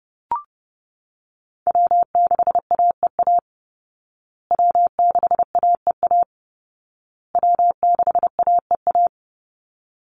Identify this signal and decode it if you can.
Morse code, heard as a keyed tone, about 30 wpm, 700 Hz. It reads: W6AEA W6AEA W6AEA
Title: Call Signs Repeated 2x - 30wpm